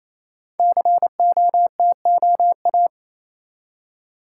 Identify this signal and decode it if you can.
Morse code, heard as a keyed tone, about 28 wpm, 700 Hz. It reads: COTOA
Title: Code Groups - 5 Letters - 28wpm